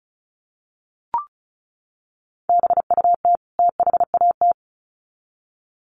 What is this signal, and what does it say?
Morse code, heard as a keyed tone, about 34 wpm, 700 Hz. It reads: BUT THAT
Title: Sets of 2 Words 35wpm